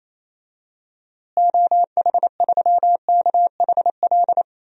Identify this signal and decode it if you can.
Morse code, heard as a keyed tone, about 28 wpm, 700 Hz. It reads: OH3KHL
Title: DXCC-100 Call Signs - 6 Characters - Repeated 3x - 28wpm